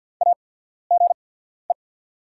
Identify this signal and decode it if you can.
Morse code, heard as a keyed tone, about 48 wpm, spaced as for 14 wpm, 690 Hz. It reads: AGE
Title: 8x Character Spacing - Top 500 Words - 50wpm